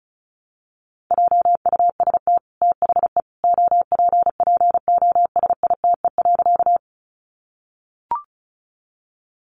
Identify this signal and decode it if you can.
Morse code, heard as a keyed tone, about 34 wpm, 700 Hz. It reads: JUST THE OPPOSITE.
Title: Sentences from Top 1300 Words 35wpm